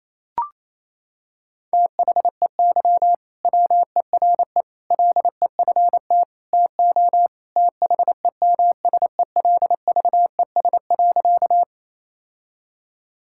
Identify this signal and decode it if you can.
Morse code, heard as a keyed tone, about 28 wpm, 700 Hz. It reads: THEY WERE LEFT TO THEMSELVES.
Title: Sentences from Top 1000 Words - Review Part 5 - 28wpm